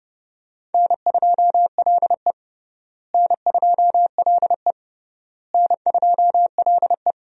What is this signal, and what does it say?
Morse code, heard as a keyed tone, about 30 wpm, 700 Hz. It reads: N2LE N2LE N2LE
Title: USA Call Signs - 4 Characters - Repeated 3x - 30wpm